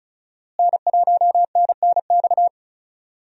Morse code, heard as a keyed tone, about 34 wpm, 700 Hz. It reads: N1NNX
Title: USA Call Signs - 5 Characters - Repeated 3x - 35wpm